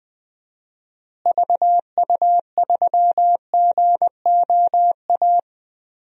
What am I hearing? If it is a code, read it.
Morse code, heard as a keyed tone, about 20 wpm, 700 Hz. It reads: VU3GOA
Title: DXCC-100 Call Signs - 6 Characters - Repeated 2x - 20wpm